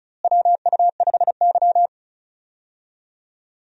Morse code, heard as a keyed tone, about 36 wpm, 700 Hz. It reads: WU5Y
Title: USA Call Signs - 4 Characters - Repeated 3x - 35wpm